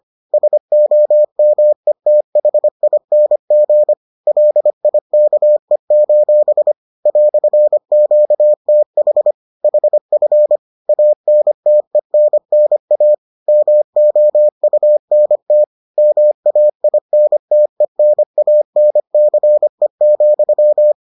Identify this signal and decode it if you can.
Morse code, heard as a keyed tone, about 25 wpm, 600 Hz. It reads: SOMETHING LIKE: "QTH HF ANTENNA MOUNT MAINTENANCE,